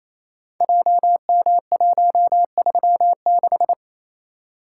Morse code, heard as a keyed tone, about 29 wpm, 700 Hz. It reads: JM136
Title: Code Groups - 5 Letter:Numbers - 28wpm